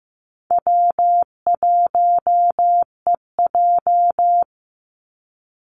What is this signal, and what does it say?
Morse code, heard as a keyed tone, about 15 wpm, 700 Hz. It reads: W1EJ
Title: NA SOTA Activator Callsigns 15wpm